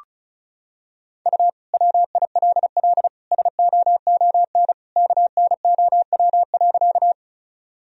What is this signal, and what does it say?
Morse code, heard as a keyed tone, about 36 wpm, 700 Hz. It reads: U WILL SOON KNOW.
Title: Sentences from Top 400 Words and Abbreviations 35wpm